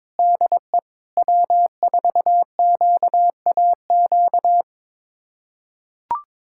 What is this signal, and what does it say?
Morse code, heard as a keyed tone, about 22 wpm, 700 Hz. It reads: DE W4QAQ
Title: USA Callsign DE Callsign - 5 Characters - 22wpm